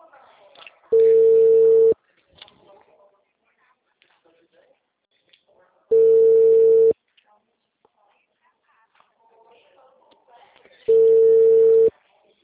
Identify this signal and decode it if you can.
Ringback tone (China, 450 Hz), with speech.